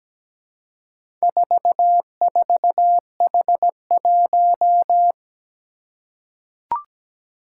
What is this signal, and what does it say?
Morse code, heard as a keyed tone, about 17 wpm, 700 Hz. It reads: 44H1
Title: Four Challenging Letter-Number Combinations 17wpm